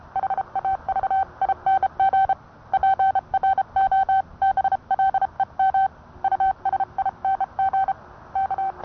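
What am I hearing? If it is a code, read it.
Morse code, heard as a keyed tone, about 30 wpm, 770 Hz. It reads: HAVING PROBLEM USING K
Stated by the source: Morse code sent by N1EA on a Vibroplex key.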